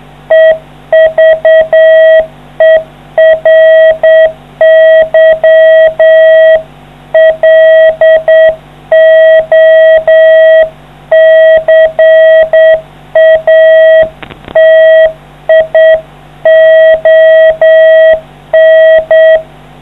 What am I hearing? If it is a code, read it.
Morse code, heard as a keyed tone, about 8 wpm, 640 Hz. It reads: EVERY LOCATION